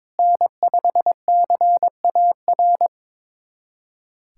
Morse code, heard as a keyed tone, about 22 wpm, 700 Hz. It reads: N5CAR
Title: USA Callsign DE Callsign 22wpm